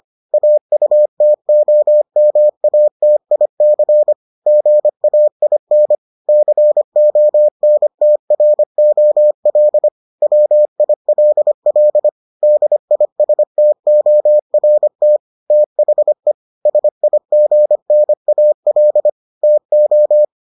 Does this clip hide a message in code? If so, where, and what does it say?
Yes: Morse code from 0.3 to 20.2 s, heard as a keyed tone, about 25 wpm, 600 Hz. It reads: AUTOMATIC GAIN CONTROL WILL DISTORT THE SIGNAL TO